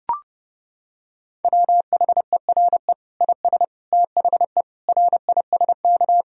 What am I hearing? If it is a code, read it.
Morse code, heard as a keyed tone, about 30 wpm, 700 Hz. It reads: WHERE IS THE RISK
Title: Sentences from Top 1000 Words - Review Part 4 - 30wpm